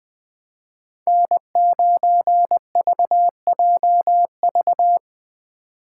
Morse code, heard as a keyed tone, about 21 wpm, 700 Hz. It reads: N9VJV
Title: USA Call Signs - 5 Characters - 6x Character Spacing - 20wpm